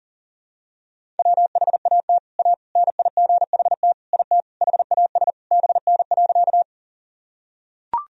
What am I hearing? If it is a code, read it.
Morse code, heard as a keyed tone, about 40 wpm, 700 Hz. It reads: WHAT A NIGHT IT HAS BN.
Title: Sentences from Top 300 Words and Abbreviations 40wpm